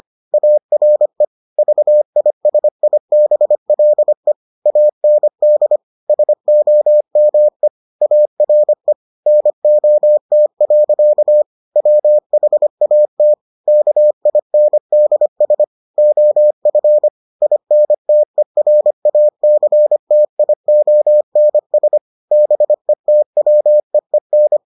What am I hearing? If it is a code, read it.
Morse code, heard as a keyed tone, about 25 wpm, 600 Hz. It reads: ARE VISIBLE AND SOME ARE NOT. WHAT KINDS OF INTERACTIONS BETWEEN